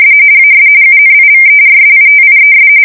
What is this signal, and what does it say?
RTTY, about 45.45 baud. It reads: AS UNIVERSITIES
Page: 170 Shift 2125/2295